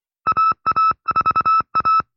Morse code, heard as a keyed tone, about 25 wpm, 1310 Hz. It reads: AA4A